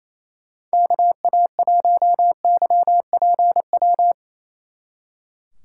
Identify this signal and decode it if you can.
Morse code, heard as a keyed tone, about 33 wpm, 700 Hz. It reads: KA1YPW
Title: Speed Racing - USA 2x3 Call Signs 28wpm